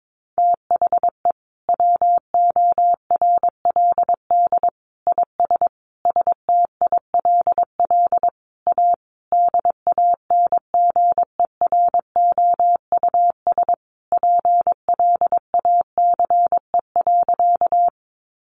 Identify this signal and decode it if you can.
Morse code, heard as a keyed tone, about 22 wpm, 700 Hz. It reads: THE WORLD IS STILL A DANGEROUS PLACE.
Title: Sentences from Top 1200 Words 22wpm